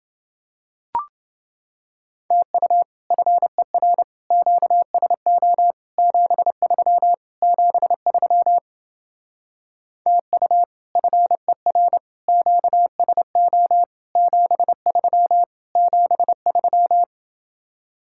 Morse code, heard as a keyed tone, about 29 wpm, 700 Hz. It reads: TU FER QSO 73 73 TU FER QSO 73 73